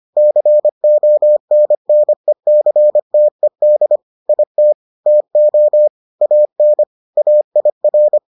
Morse code, heard as a keyed tone, about 25 wpm, 600 Hz. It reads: CONNECTED IT TO AN AIR